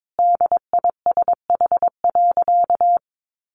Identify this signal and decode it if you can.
Morse code, heard as a keyed tone, about 22 wpm, 700 Hz. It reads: DISH.
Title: Sentences from Top 1300 Words 22wpm